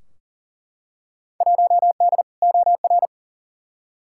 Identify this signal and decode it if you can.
Morse code, heard as a keyed tone, about 40 wpm, 700 Hz. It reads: 1D OR
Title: Field Day Exchanges - 40wpm